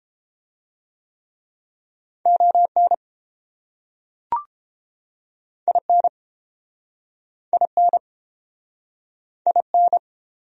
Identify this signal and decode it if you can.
Morse code, heard as a keyed tone, about 32 wpm, 700 Hz. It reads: ON IN IN IN